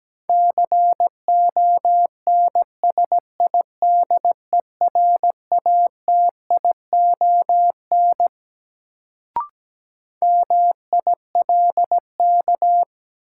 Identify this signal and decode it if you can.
Morse code, heard as a keyed tone, about 17 wpm, 700 Hz. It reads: CONSIDERATION MILK
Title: Sentences from Top 1500 Words - New Word Drill - 17wpm